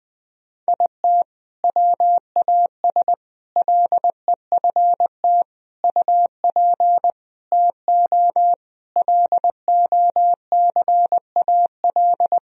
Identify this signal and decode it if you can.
Morse code, heard as a keyed tone, about 21 wpm, 700 Hz. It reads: IT WAS LEFT UP TO LOCAL
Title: Sentences from Top 1000 Words - Review Part 4 - 20wpm